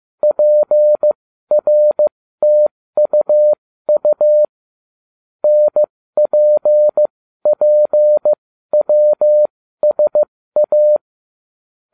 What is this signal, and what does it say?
Morse code, heard as a keyed tone, about 15 wpm, spaced as for 13 wpm, 600 Hz. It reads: PRTUU NPPWSA